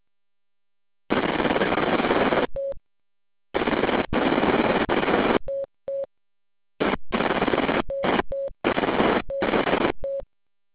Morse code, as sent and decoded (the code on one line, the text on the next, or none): none